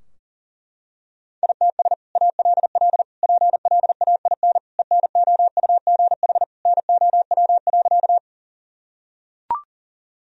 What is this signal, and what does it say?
Morse code, heard as a keyed tone, about 40 wpm, 700 Hz. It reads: ITS ALL PLAIN ENOUGH NOW.
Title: Sentences from Top 1000 Words - Review Part 2 - 40wpm